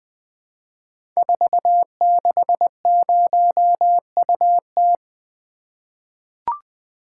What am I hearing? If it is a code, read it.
Morse code, heard as a keyed tone, about 20 wpm, 700 Hz. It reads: 460UT
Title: Code Groups - 5 Letter:Numbers - 20wpm